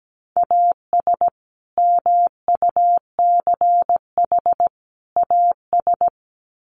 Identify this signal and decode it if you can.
Morse code, heard as a keyed tone, about 17 wpm, 700 Hz. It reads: AS MUCH AS
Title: Sets of 3 Words 17wpm